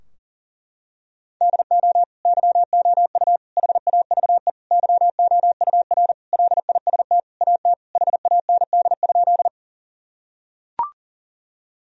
Morse code, heard as a keyed tone, about 40 wpm, 700 Hz. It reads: DO YOU HAVE YOUR LIST AT HAND?
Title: Sentences from Top 1000 Words - Review Part 2 - 40wpm